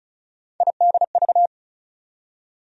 Morse code, heard as a keyed tone, about 36 wpm, 700 Hz. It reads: IDV
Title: USA Call Sign Postfixes - 3 Letters 35wpm